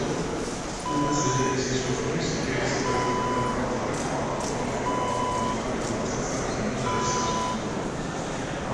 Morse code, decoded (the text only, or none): none